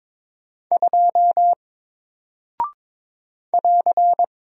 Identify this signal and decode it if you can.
Morse code, heard as a keyed tone, about 21 wpm, 700 Hz. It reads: 2 <AR>
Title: Single Character (Letter:Number:Punctuation:Prosign) Alt 22wpm